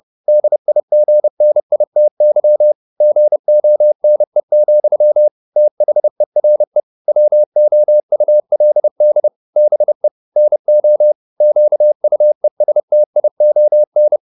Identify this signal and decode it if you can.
Morse code, heard as a keyed tone, about 30 wpm, 600 Hz. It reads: DIGNITY GONE, THERE WOULD BE NO QUESTION